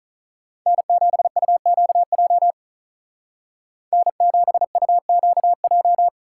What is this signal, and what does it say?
Morse code, heard as a keyed tone, about 43 wpm, 700 Hz. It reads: N7UQJ N7UQJ
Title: Speed Racing - USA Call Signs 35wpm